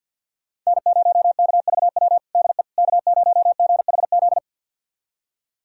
Morse code, heard as a keyed tone, about 53 wpm, 700 Hz. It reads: N0KVW DE K0GHZ
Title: USA Callsign DE Callsign - 5 Characters - 50wpm